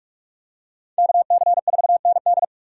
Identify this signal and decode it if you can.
Morse code, heard as a keyed tone, about 41 wpm, 700 Hz. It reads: KC4ND